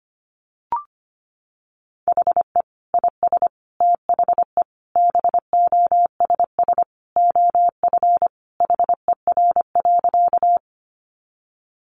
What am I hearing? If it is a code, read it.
Morse code, heard as a keyed tone, about 25 wpm, 700 Hz. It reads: HE IS THE BOSS OF HER.
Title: Sentences from Top 700 Words 25wpm